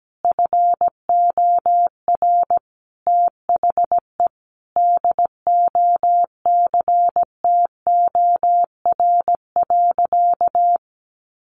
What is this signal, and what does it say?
Morse code, heard as a keyed tone, about 17 wpm, 700 Hz. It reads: FOR THE DOCTOR.
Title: Sentences from Top 1500 Words 17wpm